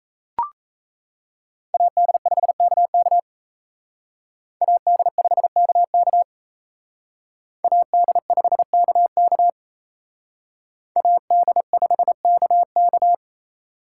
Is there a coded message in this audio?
Yes: Morse code from 1.7 to 13.1 s, heard as a keyed tone, about 36 wpm, 700 Hz. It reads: AD5KK AD5KK AD5KK AD5KK